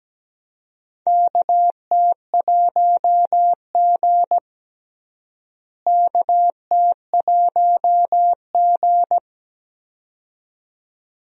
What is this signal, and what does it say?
Morse code, heard as a keyed tone, about 17 wpm, 700 Hz. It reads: KT1G KT1G
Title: USA Call Signs - 4 Characters - Repeated 2x - 17wpm